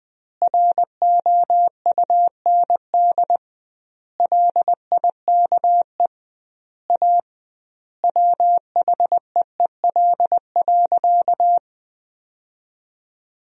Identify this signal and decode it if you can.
Morse code, heard as a keyed tone, about 20 wpm, 700 Hz. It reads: ROUND LIKE A WHEEL.
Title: 2x Word-Spacing - Sentences from Top 500 Words - 20wpm